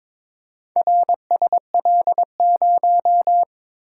Morse code, heard as a keyed tone, about 22 wpm, 700 Hz. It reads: RSL0
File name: practice / 22wpm / Four Letter-Number Combinations - 22wpm